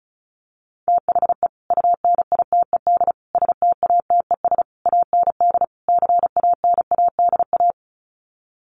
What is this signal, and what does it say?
Morse code, heard as a keyed tone, about 34 wpm, 700 Hz. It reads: THE UNITED STATES AND CANADA
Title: Sets of 5 Words 35wpm